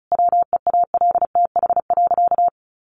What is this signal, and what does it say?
Morse code, heard as a keyed tone, about 34 wpm, 700 Hz. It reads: WEALTH.
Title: Sentences from Top 1200 Words 35wpm